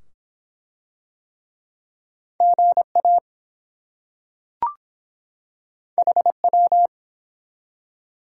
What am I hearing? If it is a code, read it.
Morse code, heard as a keyed tone, about 22 wpm, 700 Hz. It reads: GA HW